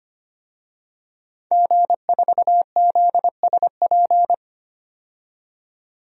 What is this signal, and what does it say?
Morse code, heard as a keyed tone, about 25 wpm, 700 Hz. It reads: G4ZSP
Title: DXCC-100 Call Signs - 5 Characters - 25wpm